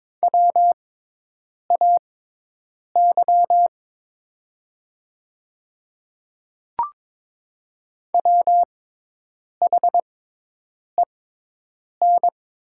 Morse code, heard as a keyed tone, about 22 wpm, spaced as for 8 wpm, 700 Hz. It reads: WAY WHEN
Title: 6x Character Spacing - Top 100 Words - 22wpm